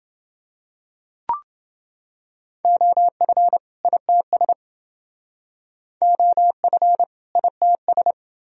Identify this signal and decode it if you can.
Morse code, heard as a keyed tone, about 29 wpm, 700 Hz. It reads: OF ITS OF ITS